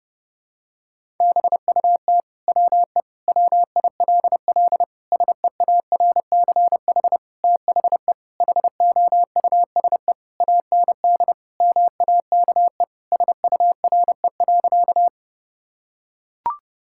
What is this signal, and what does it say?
Morse code, heard as a keyed tone, about 32 wpm, 700 Hz. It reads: BUT WE WILL SEARCH THE HOUSE AND MAKE SURE.
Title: Sentences from Top 1000 Words - Review Part 5 - 30wpm